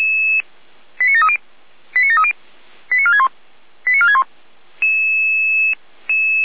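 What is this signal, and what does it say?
Five-tone selective call (ZVEI), heard as quick runs of five tones, tones of about 70 ms. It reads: 89730 89730 89461 89461
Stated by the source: Five-tone ZVEI selective call received by SDR.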